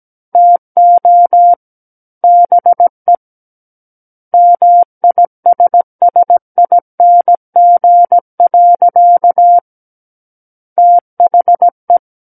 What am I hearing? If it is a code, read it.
Morse code, heard as a keyed tone, about 17 wpm, 700 Hz. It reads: TO BE MISSING. THE
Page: SKCC Learning Center - Morse Code Practice Files